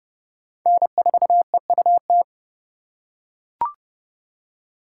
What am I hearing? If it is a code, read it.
Morse code, heard as a keyed tone, about 31 wpm, 700 Hz. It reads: N4EUT
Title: USA Call Signs - 5 Characters - 3x Character Spacing - 30wpm